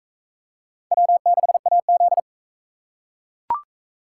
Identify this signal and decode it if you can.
Morse code, heard as a keyed tone, about 38 wpm, 700 Hz. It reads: W6AZ